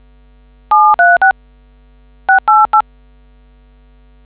DTMF, heard as telephone touch tones, tones of about 170 ms. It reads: 736 688